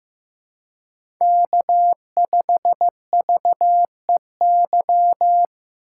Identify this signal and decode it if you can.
Morse code, heard as a keyed tone, about 15 wpm, 700 Hz. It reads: K5VEY
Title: USA Call Signs - 5 Characters - 6x Character Spacing - 15wpm